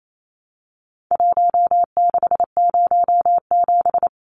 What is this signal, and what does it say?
Morse code, heard as a keyed tone, about 28 wpm, 700 Hz. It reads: 1607
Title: Numbers - Contest Style - 4-Digit 28wpm